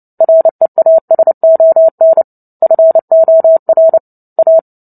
Morse code, heard as a keyed tone, about 30 wpm, 650 Hz. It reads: REASON FOR A
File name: Book Two / 03-Chapter 3 - 29 WPM